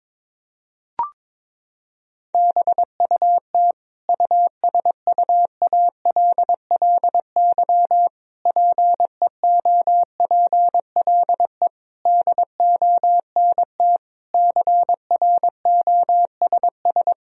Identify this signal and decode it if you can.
Morse code, heard as a keyed tone, about 22 wpm, 700 Hz. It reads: BUT USUALLY PEOPLE DONT CROSS
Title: Sentences from Top 1000 Words - Review Part 3 - 22wpm